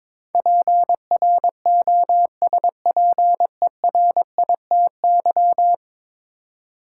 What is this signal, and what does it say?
Morse code, heard as a keyed tone, about 22 wpm, 700 Hz. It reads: PROSPERITY
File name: practice / 22wpm / 10 Letter Words - First 3 Letters Given - 22wpm